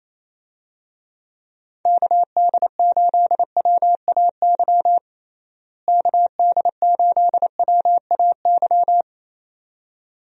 Morse code, heard as a keyed tone, about 27 wpm, 700 Hz. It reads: KD8WAY KD8WAY
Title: USA Call Signs - 6 Characters - Repeated 2x - 28wpm